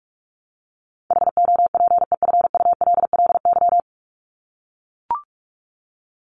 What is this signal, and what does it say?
Morse code, heard as a keyed tone, about 44 wpm, 700 Hz. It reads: HOPEFULLY
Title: Top 1251-1500 Words 45wpm